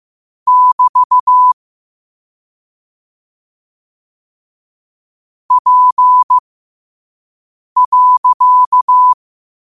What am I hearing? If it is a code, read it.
Morse code, heard as a keyed tone, about 15 wpm, spaced as for 5 wpm, 1000 Hz. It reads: <BT> P.
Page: Quotes for Fri, 15 Aug 2025 in Morse Code at 5 words per minute.